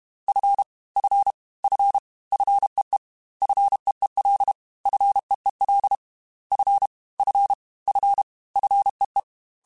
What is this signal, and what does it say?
Morse code, heard as a keyed tone, about 32 wpm, 800 Hz. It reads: F F F FEE FEEL FEEL F F F FEE
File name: F and L / F and L mp3 / F and L Prefixes 3x2 Twice @30